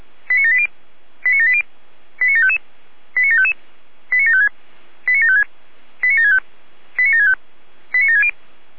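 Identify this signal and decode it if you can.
Five-tone selective call (ZVEI), heard as quick runs of five tones, tones of about 70 ms. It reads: 89780 89780 89755 89755 89756 89756 89765 89765 89760
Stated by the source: ZVEI five-tone selective call received on SDR.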